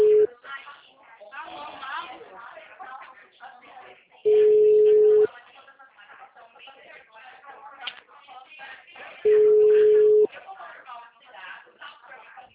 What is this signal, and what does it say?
Ringback tone (Europe, 425 Hz), with speech.
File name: out- -2027